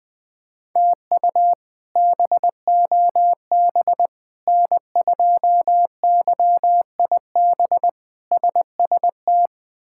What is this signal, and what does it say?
Morse code, heard as a keyed tone, about 20 wpm, 700 Hz. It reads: TU BOB N2YIB SST
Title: K1USN SST QSO Practice - 20wpm